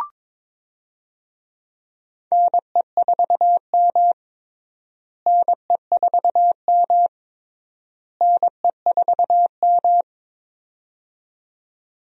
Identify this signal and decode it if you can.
Morse code, heard as a keyed tone, about 22 wpm, 700 Hz. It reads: NE4M NE4M NE4M
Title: USA Call Signs - 4 Characters - Repeated 3x - 22wpm